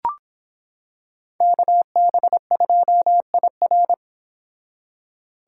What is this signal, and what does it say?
Morse code, heard as a keyed tone, about 26 wpm, 700 Hz. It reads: KB2IR